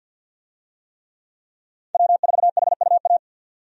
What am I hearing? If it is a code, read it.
Morse code, heard as a keyed tone, about 53 wpm, 700 Hz. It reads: W4HUA
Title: USA Call Signs - 5 Characters - 50wpm